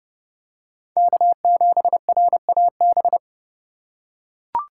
Morse code, heard as a keyed tone, about 29 wpm, 700 Hz. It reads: K7RAB